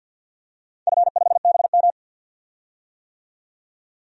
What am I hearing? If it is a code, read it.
Morse code, heard as a keyed tone, about 51 wpm, 700 Hz. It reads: F5BM
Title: DXCC-100 Call Signs - 4 Characters - 50wpm